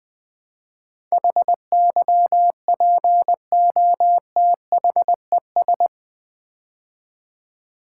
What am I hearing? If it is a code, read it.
Morse code, heard as a keyed tone, about 20 wpm, 700 Hz. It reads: HYPOTHES
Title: Build A Word - 4 Letters At A Time - 10-Letter Words - 20wpm